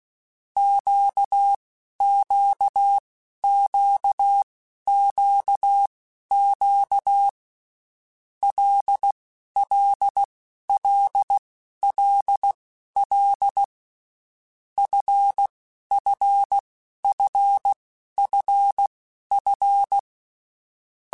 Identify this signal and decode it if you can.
Morse code, heard as a keyed tone, about 16 wpm, spaced as for 11 wpm, 800 Hz. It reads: QQQQQ LLLLL FFFFF
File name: Groups@10 / Groups@10 mp3s / LFCYQ@10